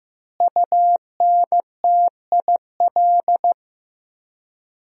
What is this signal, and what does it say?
Morse code, heard as a keyed tone, about 16 wpm, 700 Hz. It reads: UNTIL
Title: 3x Word-Spacing - Sentences from Top 500 Words - 15wpm